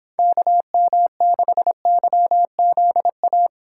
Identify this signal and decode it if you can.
Morse code, heard as a keyed tone, about 25 wpm, 700 Hz. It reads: KM6YZA